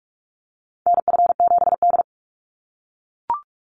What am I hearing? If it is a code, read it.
Morse code, heard as a keyed tone, about 44 wpm, 700 Hz. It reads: NF7D
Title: Call Signs - 45wpm